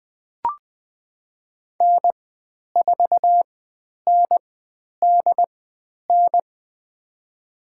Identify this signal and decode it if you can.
Morse code, heard as a keyed tone, about 20 wpm, spaced as for 10 wpm, 700 Hz. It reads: N4NDN
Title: 102 Lesson 17g - Letter u - Call Signs 20wpm